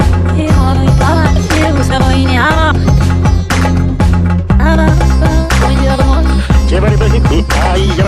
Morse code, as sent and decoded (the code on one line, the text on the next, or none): none